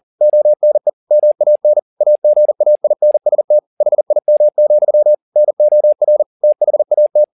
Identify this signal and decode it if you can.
Morse code, heard as a keyed tone, about 40 wpm, 600 Hz. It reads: ONE MAN AGAINST HIM, NOR THAT